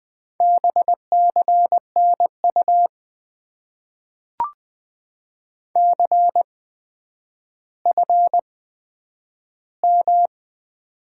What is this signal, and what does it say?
Morse code, heard as a keyed tone, about 20 wpm, spaced as for 10 wpm, 700 Hz. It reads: BCNU CFM
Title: Top 90 QSO Elements - 8x Character Spacing - 20wpm